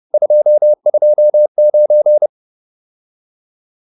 Morse code, heard as a keyed tone, about 30 wpm, 600 Hz. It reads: 229